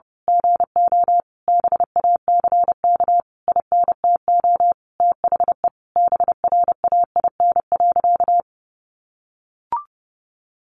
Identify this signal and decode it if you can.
Morse code, heard as a keyed tone, about 30 wpm, 700 Hz. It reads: GO BACK INTO THE BRAIN.
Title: A Sound Of Thunder 30wpm